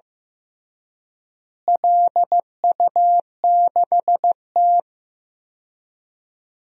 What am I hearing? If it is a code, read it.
Morse code, heard as a keyed tone, about 15 wpm, 700 Hz. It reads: LU6T
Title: DXCC-100 Call Signs - 4 Characters - Repeated 2x - 15wpm